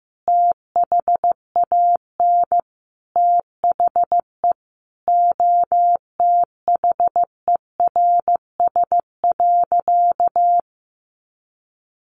Morse code, heard as a keyed tone, about 15 wpm, 700 Hz. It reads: THAN THE OTHERS.
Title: Sentences from Top 1500 Words 15wpm